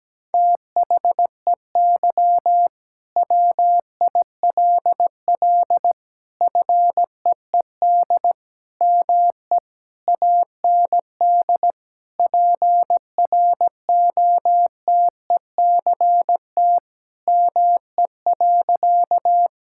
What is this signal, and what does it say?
Morse code, heard as a keyed tone, about 17 wpm, 700 Hz. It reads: THEY WILL FEED ME AND PROTECT ME.
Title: Sentences from Top 1600 Words 17wpm